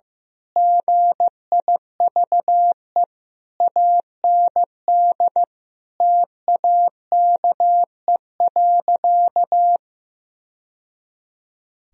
Morse code, heard as a keyed tone, about 15 wpm, 700 Hz. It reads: GIVE AND TAKE.
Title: Sentences from Top 1000 Words - Review Part 4 - 15wpm